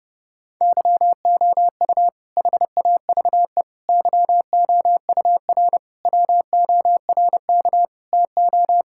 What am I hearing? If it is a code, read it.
Morse code, heard as a keyed tone, about 30 wpm, 700 Hz. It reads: YOU HAVE YOUR WORK TO
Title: Sentences from Top 1000 Words - Review Part 5 - 30wpm